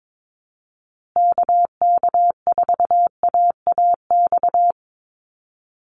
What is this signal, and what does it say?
Morse code, heard as a keyed tone, about 22 wpm, 700 Hz. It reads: KK4AAX
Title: USA Call Signs Repeated 3x 22wpm